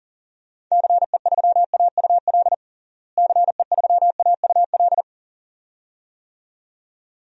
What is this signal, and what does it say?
Morse code, heard as a keyed tone, about 40 wpm, 700 Hz. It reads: CE3AUL CE3AUL
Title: DXCC-100 Call Signs - 6 Characters - Repeated 2x - 40wpm